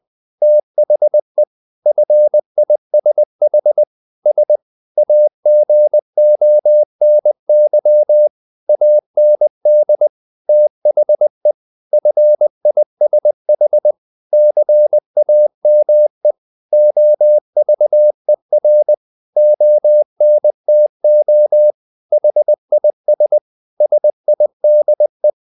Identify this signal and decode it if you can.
Morse code, heard as a keyed tone, about 20 wpm, 600 Hz. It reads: THE FISH S AGONY AND THE FISH CAME OVER ONTO HIS SIDE
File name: the old man and the sea part 2 20wpm